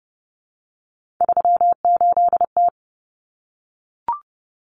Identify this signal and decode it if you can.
Morse code, heard as a keyed tone, about 30 wpm, 700 Hz. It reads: 38T
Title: Common Cut Numbers - 3-Digit 30wpm